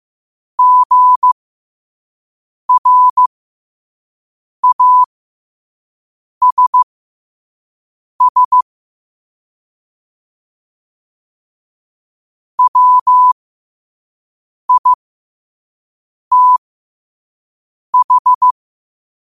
Morse code, heard as a keyed tone, about 15 wpm, spaced as for 5 wpm, 1000 Hz. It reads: GRASS WITH